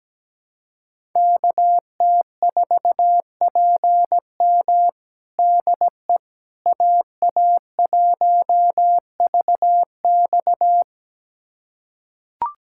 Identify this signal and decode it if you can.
Morse code, heard as a keyed tone, about 17 wpm, 700 Hz. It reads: KT4PM DE AA1VX
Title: USA Callsign DE Callsign - 5 Characters - 17wpm